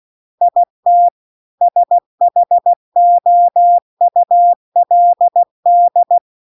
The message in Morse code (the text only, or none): IT SHOULD